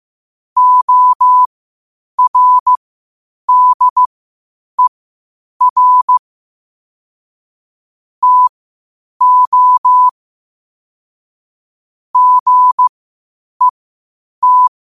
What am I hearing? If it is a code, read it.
Morse code, heard as a keyed tone, about 15 wpm, spaced as for 8 wpm, 1000 Hz. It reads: ORDER TO GET
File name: qotdfiles / 8wpm